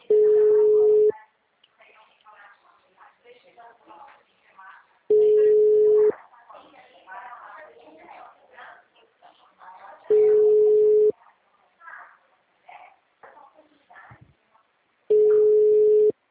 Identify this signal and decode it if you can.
Ringback tone (Europe, 425 Hz), with speech.
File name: out- -2010- - - .28